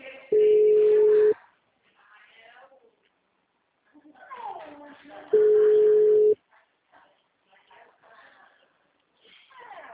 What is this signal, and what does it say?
Ringback tone (Europe, 425 Hz), with speech.